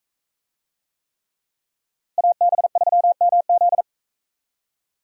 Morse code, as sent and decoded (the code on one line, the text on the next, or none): .- -... ...-- -- --..
AB3MZ